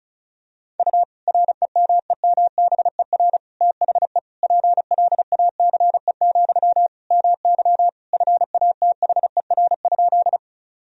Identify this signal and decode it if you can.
Morse code, heard as a keyed tone, about 34 wpm, 700 Hz. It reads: U REMEMBER THE PLACE, MY FATHER?
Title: Sentences from Top 400 Words and Abbreviations 35wpm